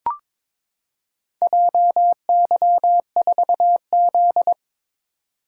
Morse code, heard as a keyed tone, about 23 wpm, 700 Hz. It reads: JY4Z
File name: practice / 22wpm / Four Letter-Number Combinations - 22wpm